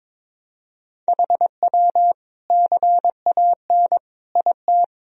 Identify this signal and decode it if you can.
Morse code, heard as a keyed tone, about 22 wpm, 700 Hz. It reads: HW CAN IT
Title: Sentences from Top 100 Words and Abbreviations 22wpm